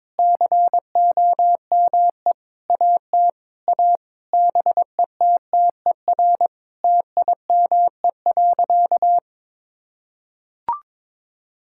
Morse code, heard as a keyed tone, about 22 wpm, 700 Hz. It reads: COME AT A BETTER TIME.
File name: practice / 22wpm / Sentences with 8-Word Phrases - 22wpm